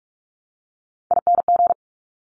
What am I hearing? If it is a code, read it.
Morse code, heard as a keyed tone, about 46 wpm, 690 Hz. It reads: ING
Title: Top Three Letter Combinations in Words 45wpm